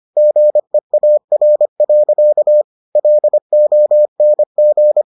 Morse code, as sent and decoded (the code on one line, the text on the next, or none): --. .- .-. .-.-.- / .-.. --- -. --.
GEAR. LONG